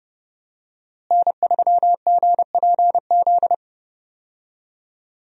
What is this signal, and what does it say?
Morse code, heard as a keyed tone, about 31 wpm, 700 Hz. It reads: N3GPZ
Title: USA Call Signs - 5 Characters - Repeated 2x - 30wpm